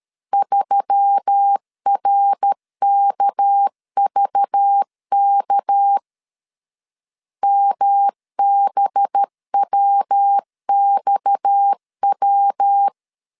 Morse code, heard as a keyed tone, about 13 wpm, 800 Hz. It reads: 3RKVK MBWXW
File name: CW / DET / 1 6 1500C S A 12wpm 3440